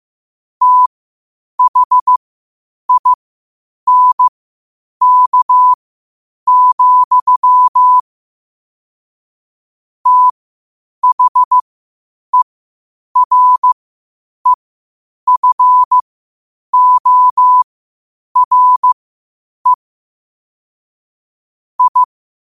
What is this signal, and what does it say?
Morse code, heard as a keyed tone, about 15 wpm, spaced as for 8 wpm, 1000 Hz. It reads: THINK, THEREFORE I
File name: qotdfiles / 8wpm